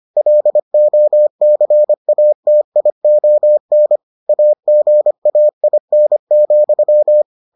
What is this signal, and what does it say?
Morse code, heard as a keyed tone, about 25 wpm, 600 Hz. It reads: LOCATION AGAIN,